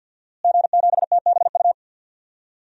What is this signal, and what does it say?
Morse code, heard as a keyed tone, about 52 wpm, 700 Hz. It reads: G7TBU
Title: DXCC-100 Call Signs - 5 Characters - Repeated 3x - 50wpm